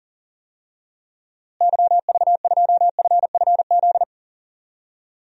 Morse code, heard as a keyed tone, about 40 wpm, 700 Hz. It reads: YV2FFZ
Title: DXCC-100 Call Signs - 6 Characters - Repeated 3x - 40wpm